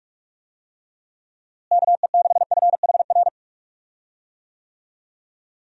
Morse code, heard as a keyed tone, about 44 wpm, 700 Hz. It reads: KE6FHR
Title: USA Call Signs - 6 Characters - Repeated 2x - 45wpm